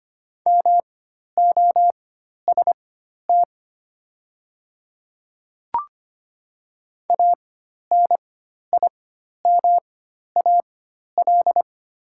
Morse code, heard as a keyed tone, about 25 wpm, spaced as for 12 wpm, 700 Hz. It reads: MOST ANIMAL
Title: 4x Character Spacing - Top 500 Words - 25wpm